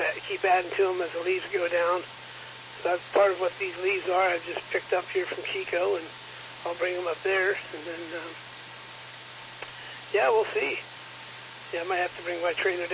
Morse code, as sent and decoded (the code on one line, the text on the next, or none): none